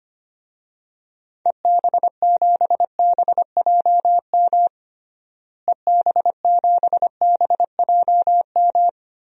Morse code, heard as a keyed tone, about 25 wpm, 700 Hz. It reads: EB7BJM EB7BJM
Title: DXCC-100 Call Signs - 6 Characters - Repeated 3x - 25wpm